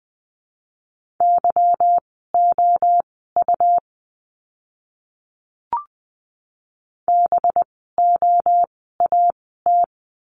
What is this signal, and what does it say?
Morse code, heard as a keyed tone, about 20 wpm, spaced as for 14 wpm, 700 Hz. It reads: YOU BOAT
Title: 185 Lesson 30f - Letter b - Words 20wpm